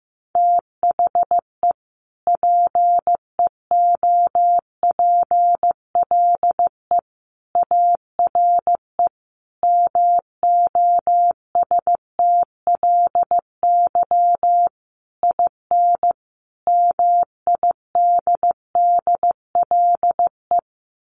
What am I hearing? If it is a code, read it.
Morse code, heard as a keyed tone, about 15 wpm, 700 Hz. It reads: THE PEOPLE ARE MOSTLY IN MIDDLE
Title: Sentences from Top 1300 Words 15wpm